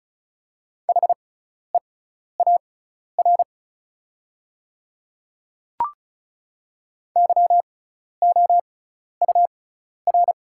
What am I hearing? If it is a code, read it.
Morse code, heard as a keyed tone, about 36 wpm, spaced as for 12 wpm, 700 Hz. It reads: HEAR YOUR
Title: 6x Character Spacing - Top 300 Words - 35wpm